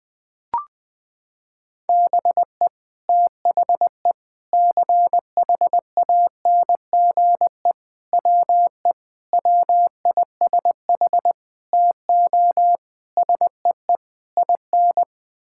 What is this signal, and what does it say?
Morse code, heard as a keyed tone, about 20 wpm, 700 Hz. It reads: BE THE CHANGE WE WISH TO SEE IN
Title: Sets of 8 Words - 20wpm